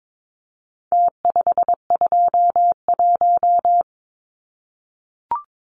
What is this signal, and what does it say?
Morse code, heard as a keyed tone, about 22 wpm, 700 Hz. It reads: T521
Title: Numbers - Contest Style - 4-Digit 22wpm